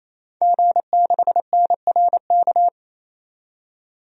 Morse code, heard as a keyed tone, about 28 wpm, 700 Hz. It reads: G6NRK
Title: DXCC-100 Call Signs - 5 Characters - 28wpm